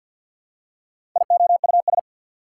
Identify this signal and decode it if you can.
Morse code, heard as a keyed tone, about 54 wpm, 690 Hz. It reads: IOUS
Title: Prefixes and Suffixes 50wpm